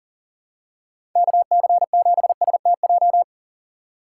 Morse code, heard as a keyed tone, about 40 wpm, 700 Hz. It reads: KC7STJ
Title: USA Call Signs - 6 Characters - 4x Character Spacing - 40wpm